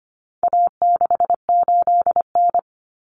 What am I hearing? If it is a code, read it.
Morse code, heard as a keyed tone, about 25 wpm, 700 Hz. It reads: A68N
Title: Common Cut Numbers - 4-Digit 25wpm